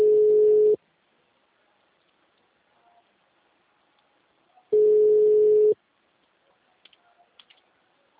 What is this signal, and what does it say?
Ringback tone (Europe, 425 Hz).
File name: out- -2009- - - .67104